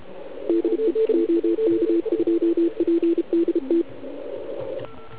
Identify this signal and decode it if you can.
Morse code, heard as a keyed tone, about 29 wpm, 340 Hz, with radio noise. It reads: DE OK2PDT
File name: Archiv / CUC129 / CUC129 pileup